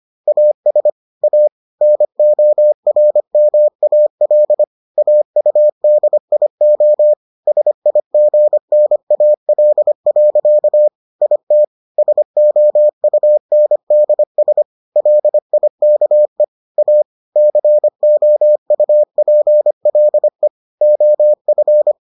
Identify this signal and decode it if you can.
Morse code, heard as a keyed tone, about 25 wpm, 600 Hz. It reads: AS A NORMAL AUDIO SIGNAL. IT SOUNDS LIKE A COUPLE OF